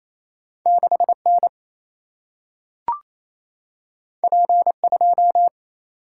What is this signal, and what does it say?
Morse code, heard as a keyed tone, about 28 wpm, 700 Hz. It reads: 6 N P 2
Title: Single-Letter Single-Number 28wpm